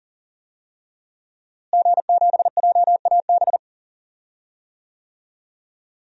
Morse code, heard as a keyed tone, about 40 wpm, 700 Hz. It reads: G7JAB
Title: DXCC-100 Call Signs - 5 Characters - Repeated 2x - 40wpm